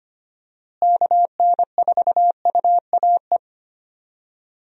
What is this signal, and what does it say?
Morse code, heard as a keyed tone, about 25 wpm, 700 Hz. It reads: KN4UAE
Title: USA Call Signs - 6 Characters - Repeated 3x - 25wpm